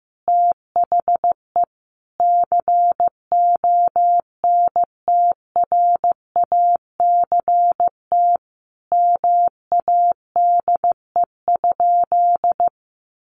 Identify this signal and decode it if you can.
Morse code, heard as a keyed tone, about 15 wpm, 700 Hz. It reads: THE CONTRACT MADE?
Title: Sentences from Top 1400 Words 15wpm